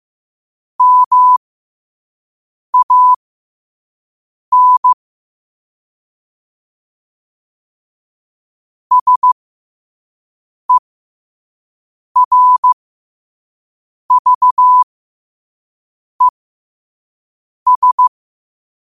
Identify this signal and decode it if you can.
Morse code, heard as a keyed tone, about 15 wpm, spaced as for 5 wpm, 1000 Hz. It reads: MAN SERVES
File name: 5wpm